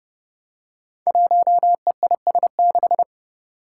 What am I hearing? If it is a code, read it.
Morse code, heard as a keyed tone, about 30 wpm, 700 Hz. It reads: 1EIS6
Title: Five Challenging Letter-Number Combinations 30wpm